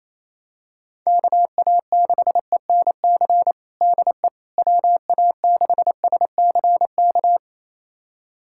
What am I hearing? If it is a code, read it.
Morse code, heard as a keyed tone, about 28 wpm, 700 Hz. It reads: KA6ENC DE WA6SCK
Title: USA Callsign DE Callsign - 6 Characters - 28wpm